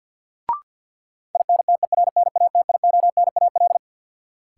Morse code, heard as a keyed tone, about 51 wpm, 700 Hz. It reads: INTERNATIONAL
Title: Sentences from Top 1000 Words - Review Part 4 - New Word Drill - 50wpm